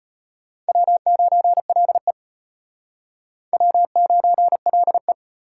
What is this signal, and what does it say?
Morse code, heard as a keyed tone, about 32 wpm, 700 Hz. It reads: W9LE W9LE